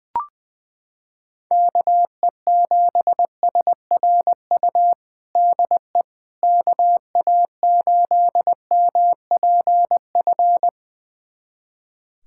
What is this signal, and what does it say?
Morse code, heard as a keyed tone, about 21 wpm, 700 Hz. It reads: KE7SRU DE KA8MPF
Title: USA Callsign DE Callsign 20wpm